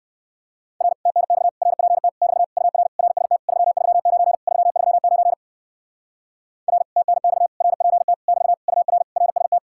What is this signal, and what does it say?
Morse code, heard as a keyed tone, about 56 wpm, 700 Hz. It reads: R TNX RPT <BT> UR RST 339 339 R TNX RPT <BT> UR RST
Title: Speed-Racing Common QSO Phrases - 45wpm